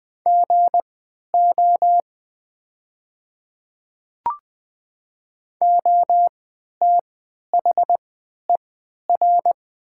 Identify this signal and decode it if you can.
Morse code, heard as a keyed tone, about 20 wpm, spaced as for 11 wpm, 700 Hz. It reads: GO OTHER
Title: 3x Character Spacing - Top 100 Words - 20wpm